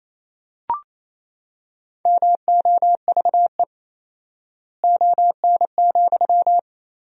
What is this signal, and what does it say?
Morse code, heard as a keyed tone, about 28 wpm, 700 Hz. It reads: MOVE ON,
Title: 4x Word-Spacing - Sentences from Top 200 Words - 28wpm